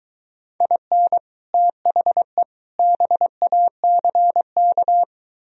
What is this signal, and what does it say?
Morse code, heard as a keyed tone, about 23 wpm, 700 Hz. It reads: IN THE BACK